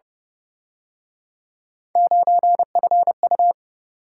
Morse code, heard as a keyed tone, about 29 wpm, 700 Hz. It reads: 9FU